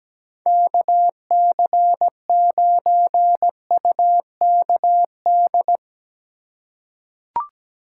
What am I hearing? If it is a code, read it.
Morse code, heard as a keyed tone, about 17 wpm, 700 Hz. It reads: KC9UKD
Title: USA Call Signs - 3x Character Spacing - 17wpm